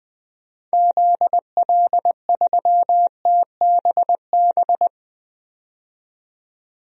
Morse code, heard as a keyed tone, about 21 wpm, 700 Hz. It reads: ZL3TBB
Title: DXCC-100 Call Signs - 6 Characters - 20wpm